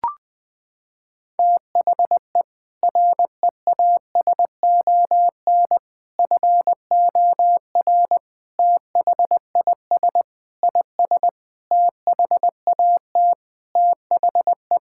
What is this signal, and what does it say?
Morse code, heard as a keyed tone, about 21 wpm, 700 Hz. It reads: THE REASON FOR THIS IS THAT THE
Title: Sentences with 7-Word Phrases - 20wpm